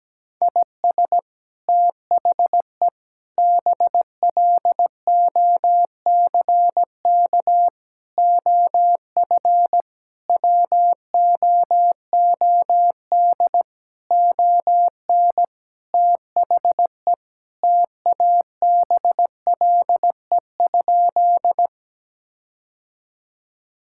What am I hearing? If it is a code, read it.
Morse code, heard as a keyed tone, about 17 wpm, 700 Hz. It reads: IS THE BLOCK OF WOOD ON THE TABLE?
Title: Sentences from Top 1600 Words 17wpm